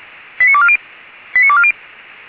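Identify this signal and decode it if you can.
Five-tone selective call (ZVEI), heard as quick runs of five tones, tones of about 70 ms. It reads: 89240 89240
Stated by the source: ZVEI five-tone selective call received on an SDR.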